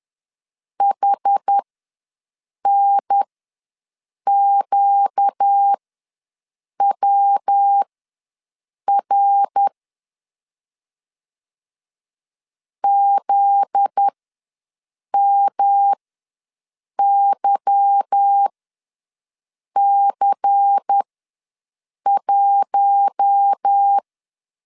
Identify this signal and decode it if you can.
Morse code, heard as a keyed tone, about 10 wpm, spaced as for 6 wpm, 800 Hz. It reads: HNQWR ZMYC1